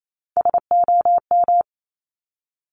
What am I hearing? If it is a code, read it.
Morse code, heard as a keyed tone, about 28 wpm, 700 Hz. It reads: SOM
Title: Top Three Letter Combinations in Words 28wpm